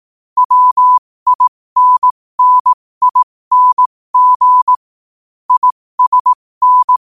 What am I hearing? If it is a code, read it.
Morse code, heard as a keyed tone, about 18 wpm, spaced as for 15 wpm, 1000 Hz. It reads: WINNING ISN